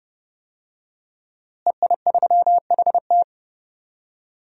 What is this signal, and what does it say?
Morse code, heard as a keyed tone, about 30 wpm, 700 Hz. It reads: EI3HT
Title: DXCC-100 Call Signs - 5 Characters - Repeated 3x - 30wpm